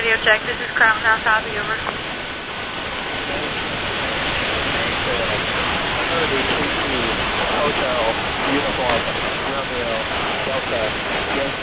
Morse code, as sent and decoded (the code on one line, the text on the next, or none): none